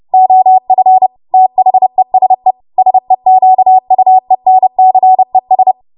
Morse code, heard as a keyed tone, about 30 wpm, 750 Hz. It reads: OF THESE SEQUENCES
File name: Morse / 30WPM